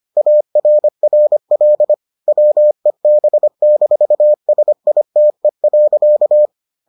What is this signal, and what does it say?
Morse code, heard as a keyed tone, about 25 wpm, 600 Hz. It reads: ARRL WEB-SITE.